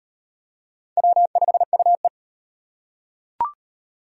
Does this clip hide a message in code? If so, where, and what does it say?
Yes: Morse code from 1.0 to 2.1 s, heard as a keyed tone, about 36 wpm, 700 Hz. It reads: W5UE